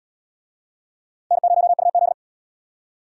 Morse code, heard as a keyed tone, about 68 wpm, 700 Hz. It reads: N0UZ